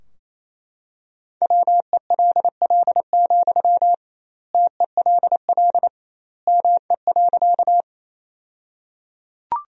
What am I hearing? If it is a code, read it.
Morse code, heard as a keyed tone, about 28 wpm, 700 Hz. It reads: WELL, TELL ME.
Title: 2x Word-Spacing - Sentences from Top 200 Words - 28wpm